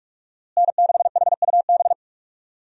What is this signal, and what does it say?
Morse code, heard as a keyed tone, about 46 wpm, 700 Hz. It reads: N6HUB
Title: USA Call Signs - 5 Characters - Repeated 3x - 45wpm